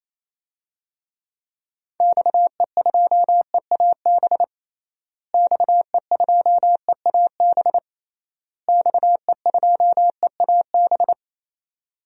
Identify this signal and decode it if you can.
Morse code, heard as a keyed tone, about 28 wpm, 700 Hz. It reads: XE2EAB XE2EAB XE2EAB
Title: DXCC-100 Call Signs - 6 Characters - Repeated 3x - 28wpm